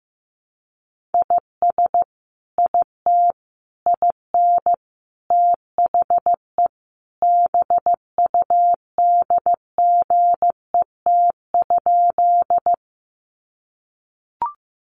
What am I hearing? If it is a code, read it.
Morse code, heard as a keyed tone, about 15 wpm, 700 Hz. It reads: IS IT IN THE BUDGET?
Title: Sentences from Top 1200 Words 15wpm